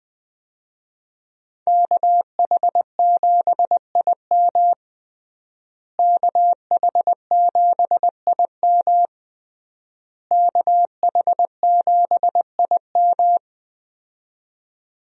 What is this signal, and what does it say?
Morse code, heard as a keyed tone, about 20 wpm, 700 Hz. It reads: KH7IM KH7IM KH7IM
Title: USA Call Signs - 5 Characters - Repeated 3x - 20wpm